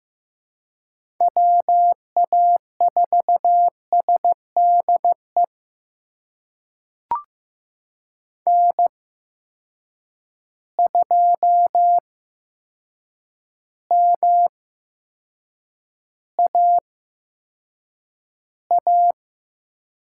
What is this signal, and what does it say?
Morse code, heard as a keyed tone, about 15 wpm, spaced as for 7 wpm, 700 Hz. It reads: WA4SDE N2MAA
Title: USA Call Signs - 8x Character Spacing - 15wpm